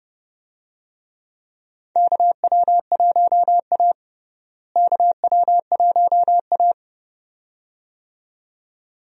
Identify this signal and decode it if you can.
Morse code, heard as a keyed tone, about 32 wpm, 700 Hz. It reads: KW1A KW1A
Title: USA Call Signs - 4 Characters - Repeated 2x - 30wpm